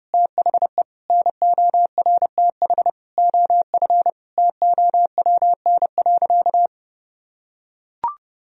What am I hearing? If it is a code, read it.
Morse code, heard as a keyed tone, about 31 wpm, 700 Hz. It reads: THE NORTH OF TOWN.
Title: Sentences from Top 1000 Words - Review Part 4 - 30wpm